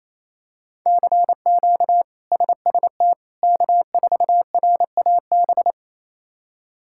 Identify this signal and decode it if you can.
Morse code, heard as a keyed tone, about 28 wpm, 700 Hz. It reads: CQ SST K4RAB
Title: K1USN SST QSO Practice - 28wpm